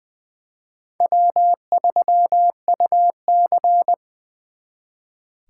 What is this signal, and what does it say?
Morse code, heard as a keyed tone, about 24 wpm, 700 Hz. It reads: W3UC
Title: Speed Racing - USA Call Signs 20wpm